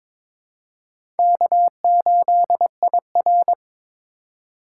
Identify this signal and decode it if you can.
Morse code, heard as a keyed tone, about 22 wpm, 700 Hz. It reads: K8IR
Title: USA Call Signs - 4 Characters - 3x Character Spacing - 22wpm